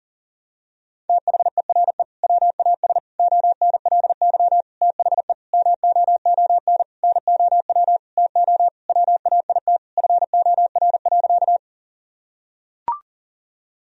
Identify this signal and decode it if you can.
Morse code, heard as a keyed tone, about 40 wpm, 700 Hz. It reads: THERE WAS ONLY THE MOON NOW TO WAIT FOR.
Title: Sentences from Top 1000 Words - Review Part 3 - 40wpm